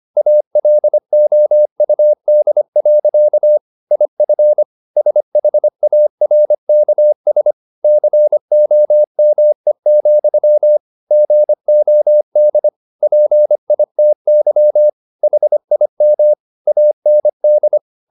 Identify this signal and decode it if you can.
Morse code, heard as a keyed tone, about 25 wpm, 600 Hz. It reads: ALOUD. IF SHARKS COME, GOD PITY HIM AND
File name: the old man and the sea part 1 25wpm